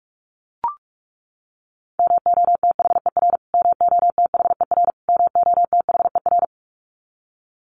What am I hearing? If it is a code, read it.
Morse code, heard as a keyed tone, about 44 wpm, 700 Hz. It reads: MOTHER MOTHER MOTHER
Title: Top 500 Words Repeated 45wpm